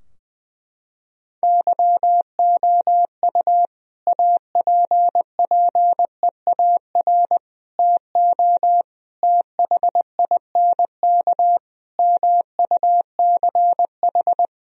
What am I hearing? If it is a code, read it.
Morse code, heard as a keyed tone, about 20 wpm, 700 Hz. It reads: YOU APPEAR TO THINK MUCH
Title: Sentences from Top 1000 Words - Review Part 2 - 20wpm